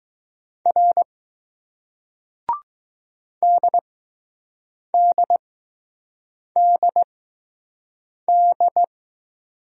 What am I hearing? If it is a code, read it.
Morse code, heard as a keyed tone, about 20 wpm, spaced as for 6 wpm, 700 Hz. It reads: R DDDD